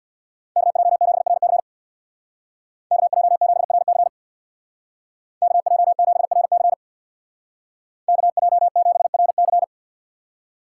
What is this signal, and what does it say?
Morse code, heard as a keyed tone, about 64 wpm, 700 Hz. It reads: KJ7RC KJ7RC KJ7RC KJ7RC